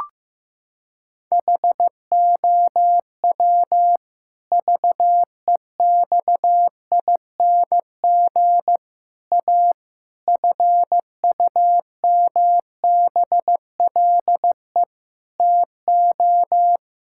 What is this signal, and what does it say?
Morse code, heard as a keyed tone, about 16 wpm, 700 Hz. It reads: HOW VEXING A FUMBLE TO
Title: Pangram Challenge v2 - 15wpm